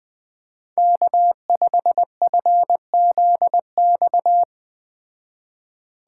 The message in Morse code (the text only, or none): K5FZX